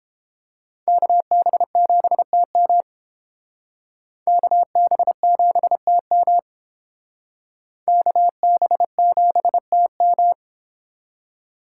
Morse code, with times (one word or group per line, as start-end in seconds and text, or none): 0.9-2.8 KB7TM
4.3-6.4 KB7TM
7.9-10.3 KB7TM